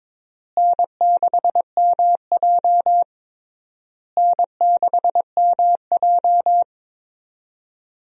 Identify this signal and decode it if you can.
Morse code, heard as a keyed tone, about 22 wpm, 700 Hz. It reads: N6MJ N6MJ
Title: USA Call Signs - 4 Characters - Repeated 3x - 22wpm